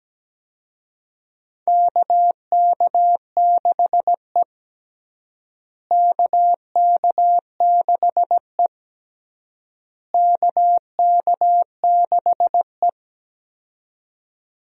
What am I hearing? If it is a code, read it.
Morse code, heard as a keyed tone, about 17 wpm, 700 Hz. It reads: KK6E KK6E KK6E
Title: USA Call Signs - 4 Characters - Repeated 3x - 17wpm